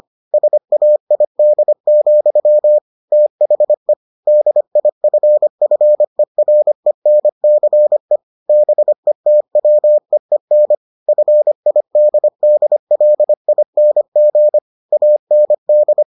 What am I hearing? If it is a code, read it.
Morse code, heard as a keyed tone, about 25 wpm, 600 Hz. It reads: SAID, THE DIFFERENCE BETWEEN FIDDLING AND